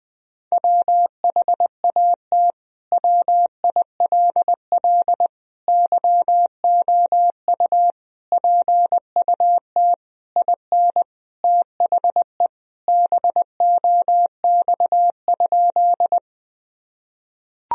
Morse code, heard as a keyed tone, about 20 wpm, 700 Hz. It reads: WHAT WILL YOU PUT IN THE BOX?
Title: Sentences from Top 1000 Words - Review Part 2 - 20wpm